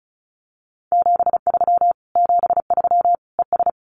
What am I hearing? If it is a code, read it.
Morse code, heard as a keyed tone, about 34 wpm, 700 Hz. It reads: 73 73 ES
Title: Common QSO Phrases Encore 35wpm